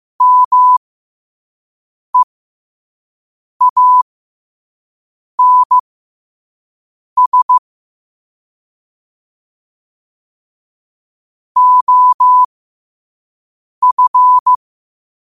Morse code, heard as a keyed tone, about 15 wpm, spaced as for 5 wpm, 1000 Hz. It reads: MEANS OF
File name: qotdfiles / 5wpm